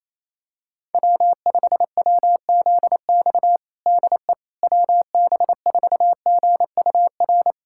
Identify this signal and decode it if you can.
Morse code, heard as a keyed tone, about 28 wpm, 700 Hz. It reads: W5WZX DE WB4GUR
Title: USA Callsign DE Callsign 28wpm